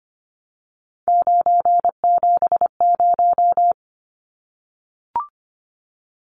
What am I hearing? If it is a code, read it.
Morse code, heard as a keyed tone, about 25 wpm, 700 Hz. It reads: 970
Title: Numbers - 3-Digit 25wpm